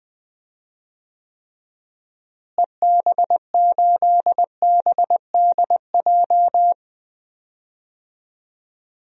Morse code, heard as a keyed tone, about 20 wpm, 700 Hz. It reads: EB8BDJ
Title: DXCC-100 Call Signs - 6 Characters - Repeated 3x - 20wpm